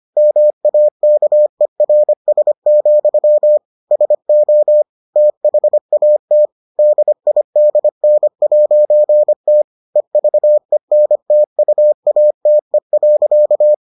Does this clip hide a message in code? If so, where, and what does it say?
Yes: Morse code from 0.2 to 13.7 s, heard as a keyed tone, about 25 wpm, 600 Hz. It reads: MAKERS, SO THAT DIDN'T EVENTUATE.